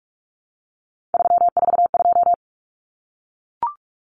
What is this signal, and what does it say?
Morse code, heard as a keyed tone, about 44 wpm, 700 Hz. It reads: 342
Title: Numbers - 3-Digit 45wpm